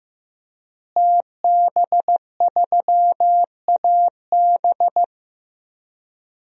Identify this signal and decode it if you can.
Morse code, heard as a keyed tone, about 15 wpm, 700 Hz. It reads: TB3AB
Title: DXCC-100 Call Signs - 5 Characters - Repeated 3x - 15wpm